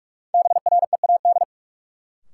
Morse code, heard as a keyed tone, about 46 wpm, 700 Hz. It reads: BREAD
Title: Sentences from Top 1600 Words - New Word Drill Alt - 45wpm